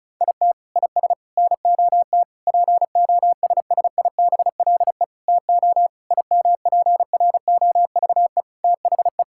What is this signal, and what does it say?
Morse code, heard as a keyed tone, about 36 wpm, 700 Hz. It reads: IT IS NOT POSSIBLE TO IMPROVE THE
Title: Sentences from Top 1000 Words - Review Part 5 - 35wpm